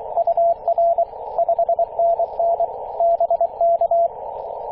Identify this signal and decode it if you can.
Morse code, heard as a keyed tone, about 24 wpm, 680 Hz, with radio noise. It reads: UR 5NN BK
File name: posts / 2016-may-04-2203z-e6ac-21-030mhz-cw